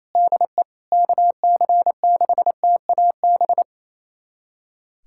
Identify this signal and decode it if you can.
Morse code, heard as a keyed tone, about 28 wpm, 700 Hz. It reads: DE KC6TAB
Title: USA Callsign DE Callsign 28wpm